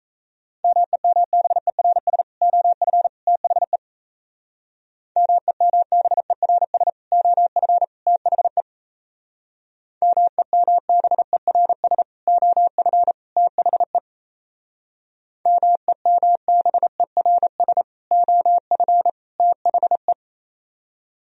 Morse code, heard as a keyed tone, about 34 wpm, 700 Hz. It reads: MEMBERS OF THE MEMBERS OF THE MEMBERS OF THE MEMBERS OF THE
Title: Speed-Racing - Sets of 3 Words 28wpm